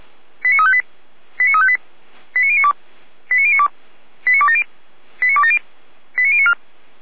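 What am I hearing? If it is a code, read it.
Five-tone selective call (ZVEI), heard as quick runs of five tones, tones of about 70 ms. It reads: 89358 89358 89092 89092 89270 89270 89094
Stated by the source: ZVEI five-tone call 89262 received by SDR, from feuerwehr-orlamuende.org fire brigade audio server.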